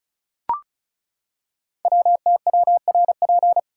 Morse code, heard as a keyed tone, about 36 wpm, 700 Hz. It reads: WTWRP
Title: Code Groups - 5 Letters - 35wpm